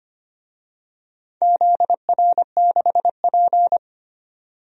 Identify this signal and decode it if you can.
Morse code, heard as a keyed tone, about 25 wpm, 700 Hz. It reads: ZR6P
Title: DXCC-100 Call Signs - 4 Characters - 25wpm